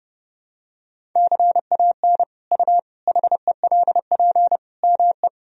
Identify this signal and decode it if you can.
Morse code, heard as a keyed tone, about 32 wpm, 700 Hz. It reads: CAN U HELP ME
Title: Sentences from Top 200 Words and Abbreviations 30wpm